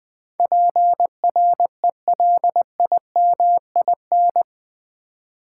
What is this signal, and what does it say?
Morse code, heard as a keyed tone, about 20 wpm, 700 Hz. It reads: PRELIMIN
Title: Build A Word - 4 Letters At A Time - 11-Letter Words - 20wpm